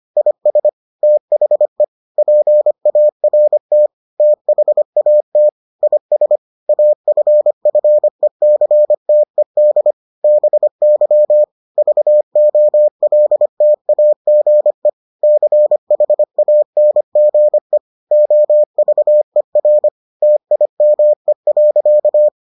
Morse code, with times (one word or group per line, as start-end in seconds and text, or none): 0.2-0.7 IS
1.0-1.8 THE
2.2-3.9 PART
4.2-5.5 THAT
5.8-6.4 IS
6.7-9.9 AFFECTED
10.2-11.4 BY
11.8-14.9 VOLTAGE
15.2-17.8 CHANGE
18.1-19.9 OVER
20.2-22.3 TIME.